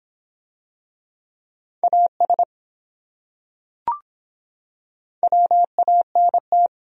Morse code, heard as a keyed tone, about 23 wpm, 700 Hz. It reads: AS WANT